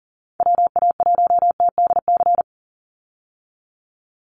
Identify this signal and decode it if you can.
Morse code, heard as a keyed tone, about 40 wpm, 700 Hz. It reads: WA1TDC
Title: USA Call Signs 40wpm